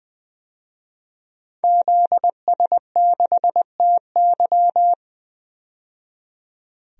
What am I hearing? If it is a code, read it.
Morse code, heard as a keyed tone, about 20 wpm, 700 Hz. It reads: ZS6TY
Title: DXCC-100 Call Signs - 5 Characters - 20wpm